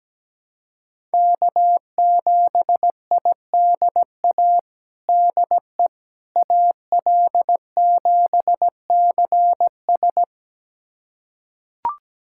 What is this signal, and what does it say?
Morse code, heard as a keyed tone, about 17 wpm, 700 Hz. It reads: K7IDA DE AL7CS
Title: USA Callsign DE Callsign - 5 Characters - 17wpm